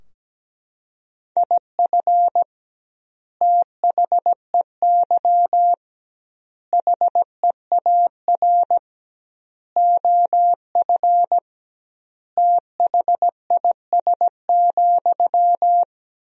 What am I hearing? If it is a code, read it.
Morse code, heard as a keyed tone, about 17 wpm, 700 Hz. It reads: IF THEY HEAR OF THIS,
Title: 2x Word-Spacing - Sentences from Top 400 Words - 17wpm